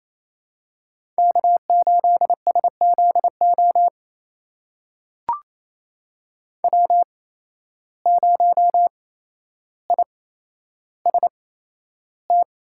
Morse code, heard as a keyed tone, about 28 wpm, spaced as for 12 wpm, 700 Hz. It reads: K8SZO W0IST
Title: USA Call Signs - 5 Characters - 8x Character Spacing - 28wpm